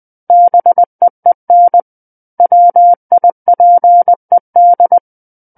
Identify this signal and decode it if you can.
Morse code, heard as a keyed tone, about 20 wpm, 700 Hz. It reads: BEEN WIPED
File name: 20 WPM